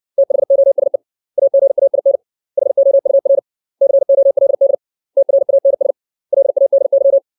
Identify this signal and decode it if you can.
Morse code, heard as a keyed tone, about 60 wpm, 550 Hz. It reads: THOSE AGAIN HOUR COLD TRANS LADY
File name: 632 words / 632 Words 60WPM